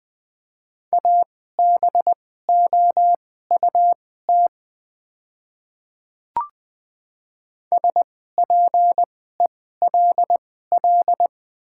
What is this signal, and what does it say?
Morse code, heard as a keyed tone, about 20 wpm, spaced as for 14 wpm, 700 Hz. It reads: ABOUT SPELL
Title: 2x Character Spacing - Top 200 Words - 20wpm